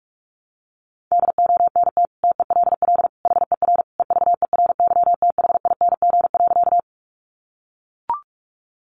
Morse code, heard as a keyed tone, about 44 wpm, 700 Hz. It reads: DONT TELL HER EVERYTHING.
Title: Sentences from Top 700 Words 45wpm